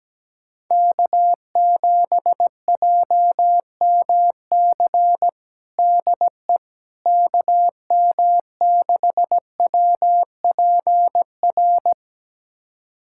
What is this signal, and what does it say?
Morse code, heard as a keyed tone, about 17 wpm, 700 Hz. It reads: K7JMC DE KM6WPR
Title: USA Callsign DE Callsign 17wpm